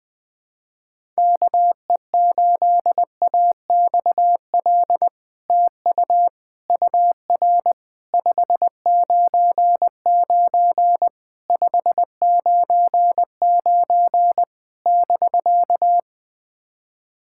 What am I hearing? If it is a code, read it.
Morse code, heard as a keyed tone, about 20 wpm, 700 Hz. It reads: KE8AXL TU UR 599 599 <BK>
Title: SOTA QSO Practice 20wpm